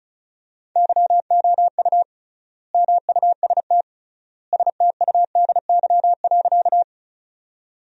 Morse code, heard as a keyed tone, about 36 wpm, 700 Hz. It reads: YOU MUST STUDY.
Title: 3x Word-Spacing - Sentences from Top 300 Words - 35wpm